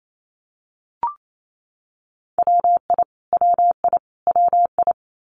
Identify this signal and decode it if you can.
Morse code, heard as a keyed tone, about 28 wpm, 700 Hz. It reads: WI WI WI
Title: US States Repeated 3x 28wpm